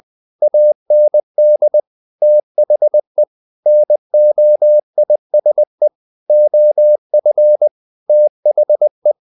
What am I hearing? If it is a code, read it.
Morse code, heard as a keyed tone, about 20 wpm, 600 Hz. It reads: AND THE NOISE OF THE
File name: the old man and the sea / the old man and the sea part 1 20wpm